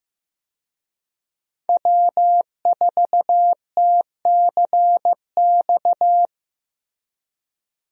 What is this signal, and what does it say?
Morse code, heard as a keyed tone, about 15 wpm, 700 Hz. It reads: W4TCX
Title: USA Call Signs - 5 Characters - Repeated 2x - 15wpm